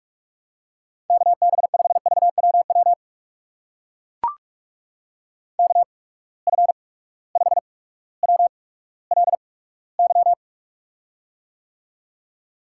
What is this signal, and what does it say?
Morse code, heard as a keyed tone, about 43 wpm, spaced as for 19 wpm, 700 Hz. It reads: KB5VWW KF5WLY
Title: USA Call Signs - 6 Characters - 8x Character Spacing - 45wpm